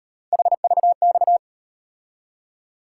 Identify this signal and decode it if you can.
Morse code, heard as a keyed tone, about 38 wpm, 700 Hz. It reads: HVX